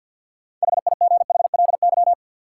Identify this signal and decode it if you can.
Morse code, heard as a keyed tone, about 53 wpm, 700 Hz. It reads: HIGHLY
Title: Sentences from Top 1100 Words - New Word Drill Alt - 50wpm